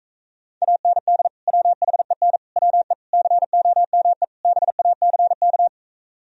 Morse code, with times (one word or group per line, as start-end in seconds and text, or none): 0.6-1.3 AND
1.5-2.4 WHEN
2.6-2.9 WE
3.1-4.2 COME
4.4-5.7 BACK